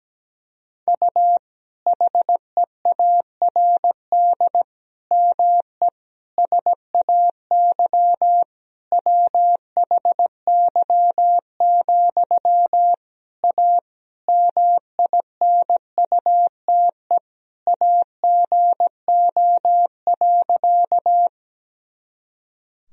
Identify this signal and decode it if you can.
Morse code, heard as a keyed tone, about 17 wpm, 700 Hz. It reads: U HEARD ME SAY WHY, A MINUTE AGO.
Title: Sentences from Top 500 Words and Abbreviations 17wpm